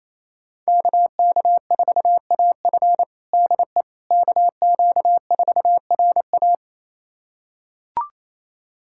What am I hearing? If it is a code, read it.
Morse code, heard as a keyed tone, about 28 wpm, 700 Hz. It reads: KK4AF DE KQ4RA
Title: USA Callsign DE Callsign - 5 Characters - 28wpm